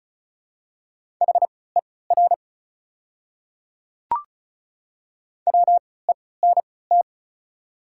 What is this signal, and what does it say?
Morse code, heard as a keyed tone, about 34 wpm, spaced as for 20 wpm, 700 Hz. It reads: HER WENT
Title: 3x Character Spacing - Top 300 Words - 35wpm